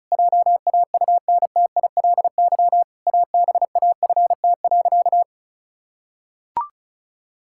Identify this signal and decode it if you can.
Morse code, heard as a keyed tone, about 36 wpm, 700 Hz. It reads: JAUNTILY ABAFT.
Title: Pangram Challenge v2 - 35wpm